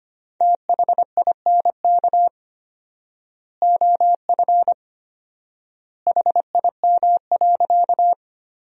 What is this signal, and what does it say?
Morse code, heard as a keyed tone, about 25 wpm, 700 Hz. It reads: THINK OF HIM.
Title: 4x Word-Spacing - Sentences from Top 200 Words - 25wpm